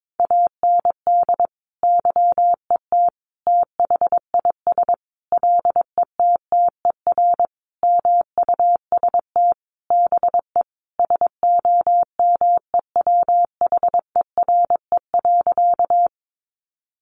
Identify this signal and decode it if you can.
Morse code, heard as a keyed tone, about 22 wpm, 700 Hz. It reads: AND YET THIS LETTER MUST BE SOMEWHERE.
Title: Sentences from Top 1400 Words 22wpm